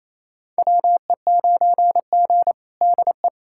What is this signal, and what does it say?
Morse code, heard as a keyed tone, about 27 wpm, 700 Hz. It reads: WE9G DE
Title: USA Callsign DE Callsign - 4 Characters - 28wpm